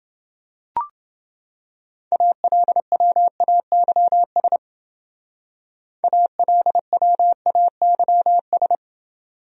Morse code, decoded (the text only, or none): ALWAYS ALWAYS